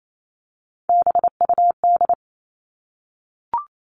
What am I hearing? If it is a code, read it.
Morse code, heard as a keyed tone, about 28 wpm, 700 Hz. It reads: BUD
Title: All Cut Numbers - 3-Digit 28wpm